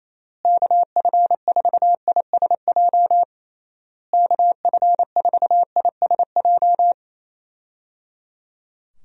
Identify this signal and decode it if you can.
Morse code, heard as a keyed tone, about 28 wpm, 700 Hz. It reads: KF4ISJ KF4ISJ
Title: USA Call Signs - 6 Characters - Repeated 3x - 28wpm